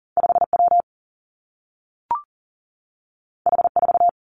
Morse code, heard as a keyed tone, about 39 wpm, 700 Hz. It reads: 5 W H 4
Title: Two Challenging Letter-Number Combinations 40wpm